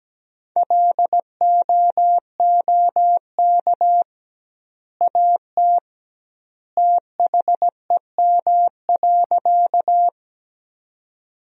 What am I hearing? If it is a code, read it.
Morse code, heard as a keyed tone, about 17 wpm, 700 Hz. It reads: LOOK AT THEM.
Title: 2x Word-Spacing - Sentences from Top 100 Words - 17wpm